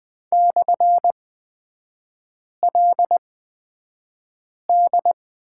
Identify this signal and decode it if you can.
Morse code, heard as a keyed tone, about 20 wpm, spaced as for 5 wpm, 700 Hz. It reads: /LD
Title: 224 Lesson 36e - BT - Three Characters 20wpm